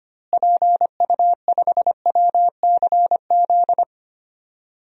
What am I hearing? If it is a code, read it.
Morse code, heard as a keyed tone, about 25 wpm, 700 Hz. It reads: PU5WCZ
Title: DXCC-100 Call Signs - 6 Characters - Repeated 3x - 25wpm